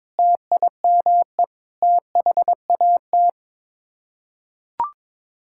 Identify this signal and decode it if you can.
Morse code, heard as a keyed tone, about 22 wpm, 700 Hz. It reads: TIME THAT
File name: practice / 22wpm / Sets of 6 Words - 22wpm